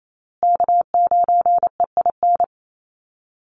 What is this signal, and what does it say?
Morse code, heard as a keyed tone, about 28 wpm, 700 Hz. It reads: K9EIN
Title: USA Call Signs Repeated 2x 28wpm